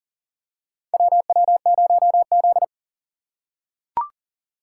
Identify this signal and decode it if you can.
Morse code, heard as a keyed tone, about 37 wpm, 700 Hz. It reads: WW0Z
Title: USA Call Signs - 4x Character Spacing - 40wpm